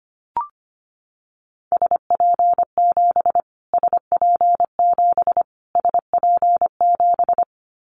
Morse code, heard as a keyed tone, about 25 wpm, 700 Hz. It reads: SP7 SP7 SP7
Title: Callsign Prefixes for Top 25 DXCC - Repeated 3x - 25wpm